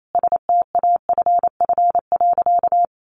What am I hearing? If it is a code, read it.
Morse code, heard as a keyed tone, about 28 wpm, 700 Hz. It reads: STAFF.
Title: Sentences from Top 1500 Words 28wpm